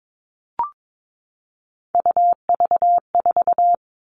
Morse code, heard as a keyed tone, about 22 wpm, 700 Hz. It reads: UV4
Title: All Cut Numbers - 3-Digit 22wpm